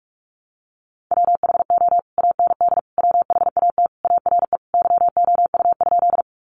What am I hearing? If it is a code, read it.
Morse code, heard as a keyed tone, about 44 wpm, 700 Hz. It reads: WHO AND WHAT ARE YOU?
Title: Sentences from Top 100 Words 45wpm